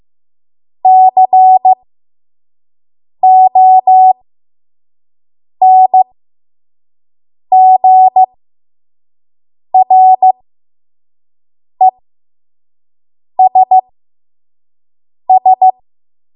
Morse code, heard as a keyed tone, about 15 wpm, spaced as for 5 wpm, 750 Hz. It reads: CONGRESS
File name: Morse / 05WPM